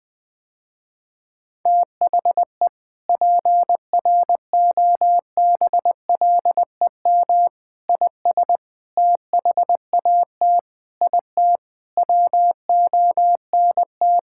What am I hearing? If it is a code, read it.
Morse code, heard as a keyed tone, about 20 wpm, 700 Hz. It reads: THE PROBLEM IS THAT IT WONT
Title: Sentences with 3-Word Phrases - Spoken Beginning - 20wpm